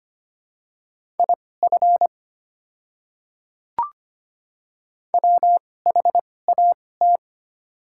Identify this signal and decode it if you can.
Morse code, heard as a keyed tone, about 25 wpm, spaced as for 18 wpm, 690 Hz. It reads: IF WHAT
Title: 2x Character Spacing - Top 100 Words - 25wpm